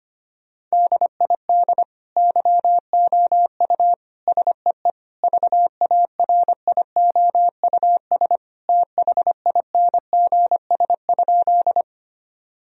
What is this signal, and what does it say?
Morse code, heard as a keyed tone, about 25 wpm, 700 Hz. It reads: DID YOU SEE VARIOUS THINGS?
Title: Sentences from Top 1000 Words - Review Part 4 - 25wpm